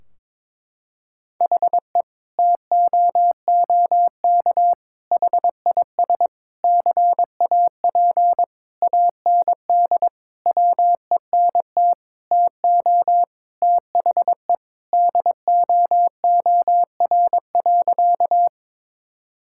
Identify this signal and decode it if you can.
Morse code, heard as a keyed tone, about 22 wpm, 700 Hz. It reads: HE TOOK HIS CAP AND WENT TO THE DOOR.
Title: Sentences from Top 1600 Words 22wpm